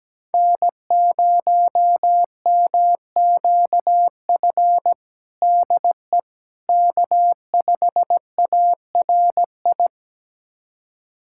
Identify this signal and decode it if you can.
Morse code, heard as a keyed tone, about 17 wpm, 700 Hz. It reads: N0MQF DE K5ARI
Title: USA Callsign DE Callsign 17wpm